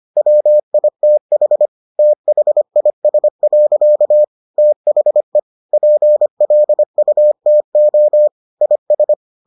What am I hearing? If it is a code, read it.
Morse code, heard as a keyed tone, about 25 wpm, 600 Hz. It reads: WITH THIS. THE PLUTO IS